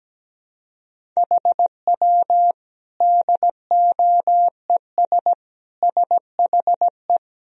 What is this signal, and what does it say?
Morse code, heard as a keyed tone, about 17 wpm, 700 Hz. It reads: HW DOES SHE
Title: Sentences from Top 200 Words and Abbreviations 17wpm